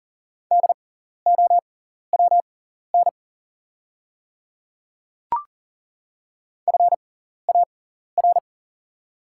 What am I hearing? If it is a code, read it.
Morse code, heard as a keyed tone, about 40 wpm, spaced as for 14 wpm, 700 Hz. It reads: DOWN FAR
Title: 6x Character Spacing - Top 500 Words - 40wpm